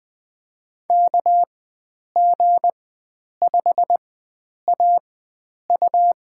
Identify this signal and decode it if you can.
Morse code, heard as a keyed tone, about 20 wpm, spaced as for 9 wpm, 700 Hz. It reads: KG5AU
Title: USA Call Signs - 5 Characters - 4x Character Spacing - 20wpm